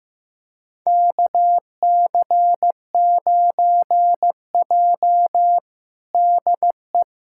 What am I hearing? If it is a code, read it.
Morse code, heard as a keyed tone, about 16 wpm, 700 Hz. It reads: KC9J DE
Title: USA Callsign DE Callsign - 4 Characters - 15wpm